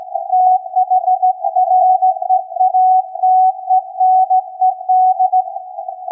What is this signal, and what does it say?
Morse code, heard as a keyed tone, about 16 wpm, 740 Hz. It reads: THREATENED
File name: iz7slz / VLF / SAQ 16-11-2022